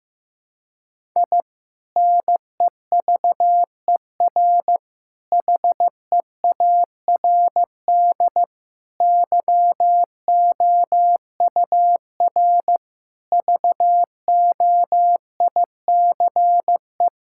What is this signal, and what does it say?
Morse code, heard as a keyed tone, about 15 wpm, 700 Hz. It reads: I NEVER HEARD YOUR VOICE
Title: Sentences from Top 1000 Words - Review Part 6 - 15wpm